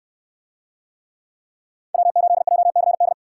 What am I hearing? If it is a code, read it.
Morse code, heard as a keyed tone, about 68 wpm, 700 Hz. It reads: W8JCD